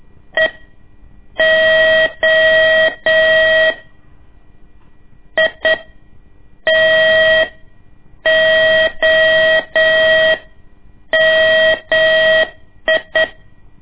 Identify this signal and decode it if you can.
Morse code, heard as a keyed tone, about 8 wpm, spaced as for 6 wpm, 640 Hz. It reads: EO ITOMI
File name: ik1weg2 / CORSI CW / lez2-1